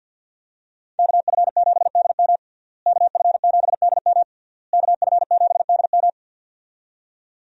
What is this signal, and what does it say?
Morse code, heard as a keyed tone, about 50 wpm, 700 Hz. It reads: KF7DM KF7DM KF7DM
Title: USA Call Signs - 5 Characters - Repeated 3x - 50wpm